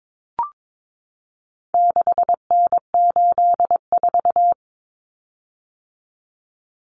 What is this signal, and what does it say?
Morse code, heard as a keyed tone, about 22 wpm, 700 Hz. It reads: 6N84
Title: Common Cut Numbers - 4-Digit 22wpm